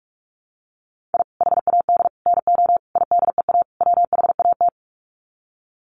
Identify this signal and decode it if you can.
Morse code, heard as a keyed tone, about 44 wpm, 700 Hz. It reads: I HAD NO IDEA WHAT
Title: Sets of 5 Words Encore 45wpm